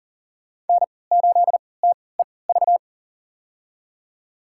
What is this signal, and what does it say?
Morse code, heard as a keyed tone, about 40 wpm, spaced as for 23 wpm, 700 Hz. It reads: N8TEV
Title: USA Call Signs - 3x Character Spacing - 40wpm